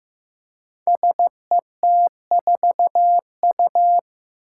Morse code, heard as a keyed tone, about 15 wpm, 700 Hz. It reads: SET4U
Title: Five Challenging Letter-Number Combinations 15wpm